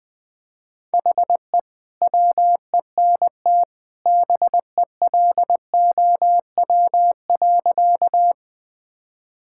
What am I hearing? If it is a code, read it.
Morse code, heard as a keyed tone, about 20 wpm, 700 Hz. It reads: HE WENT BELOW.
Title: Sentences from Top 1000 Words - Review Part 5 - 20wpm